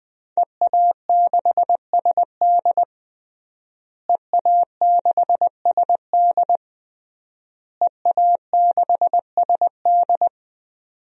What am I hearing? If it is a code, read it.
Morse code, heard as a keyed tone, about 20 wpm, 700 Hz. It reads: EA6SD EA6SD EA6SD
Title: DXCC-100 Call Signs - 5 Characters - Repeated 3x - 20wpm